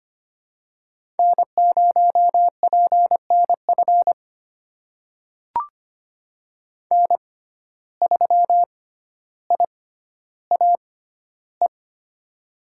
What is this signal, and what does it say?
Morse code, heard as a keyed tone, about 25 wpm, spaced as for 13 wpm, 700 Hz. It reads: N0PNF N3IAE
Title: USA Call Signs - 5 Characters - 6x Character Spacing - 25wpm